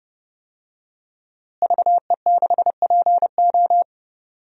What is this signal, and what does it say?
Morse code, heard as a keyed tone, about 30 wpm, 700 Hz. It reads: VE6PO
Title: DXCC-100 Call Signs - 5 Characters - Repeated 3x - 30wpm